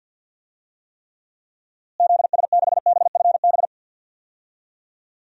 Morse code, heard as a keyed tone, about 51 wpm, 700 Hz. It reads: ZS6BFB
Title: DXCC-100 Call Signs - 6 Characters - Repeated 2x - 50wpm